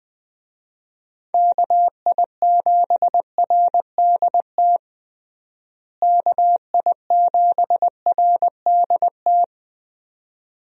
Morse code, heard as a keyed tone, about 20 wpm, 700 Hz. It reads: KI7RDT KI7RDT
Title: USA Call Signs - 6 Characters - Repeated 2x - 20wpm